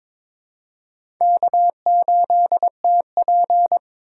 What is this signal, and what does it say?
Morse code, heard as a keyed tone, about 21 wpm, 700 Hz. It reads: K8TP
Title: USA Call Signs - 4 Characters - 2x Character Spacing - 22wpm